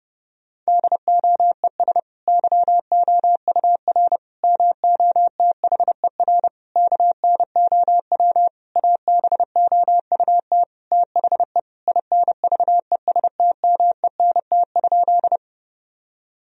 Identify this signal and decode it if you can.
Morse code, heard as a keyed tone, about 30 wpm, 700 Hz. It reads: DOES YOUR MOTHER KNOW ABOUT THE INVESTMENT?
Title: Sentences from Top 1000 Words - Review Part 4 - 30wpm